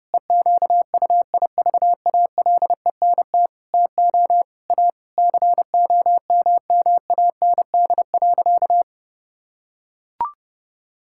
Morse code, heard as a keyed tone, about 30 wpm, 700 Hz. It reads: EQUIVALENT TO A COMMAND.
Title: Sentences from Top 1600 Words 30wpm